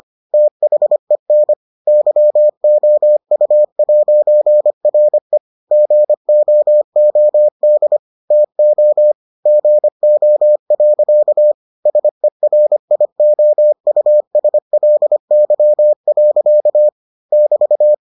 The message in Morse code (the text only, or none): THEN YOU'RE GOOD TO GO. SERIOUSLY. <BT>